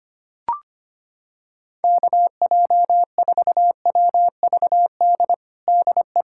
Morse code, heard as a keyed tone, about 25 wpm, 700 Hz. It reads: KJ4WVD DE
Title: USA Callsign DE Callsign - 6 Characters - 25wpm